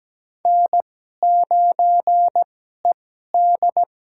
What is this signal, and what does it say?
Morse code, heard as a keyed tone, about 17 wpm, spaced as for 12 wpm, 700 Hz. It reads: N9ED
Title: USA Call Signs - 4 Characters - 2x Character Spacing - 17wpm